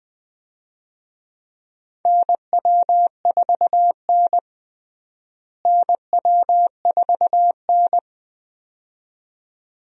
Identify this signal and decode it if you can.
Morse code, heard as a keyed tone, about 20 wpm, 700 Hz. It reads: NW4N NW4N
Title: USA Call Signs - 4 Characters - Repeated 2x - 20wpm